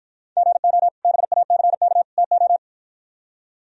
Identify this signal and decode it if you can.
Morse code, heard as a keyed tone, about 53 wpm, 700 Hz. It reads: GO BACK TO